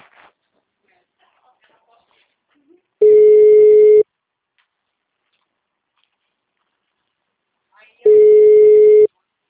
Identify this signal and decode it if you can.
Ringback tone (Europe, 425 Hz), with speech.